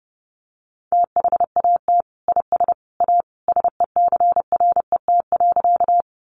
Morse code, heard as a keyed tone, about 30 wpm, 700 Hz. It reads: THAT IS A SECRET.
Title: Sentences from Top 1200 Words 30wpm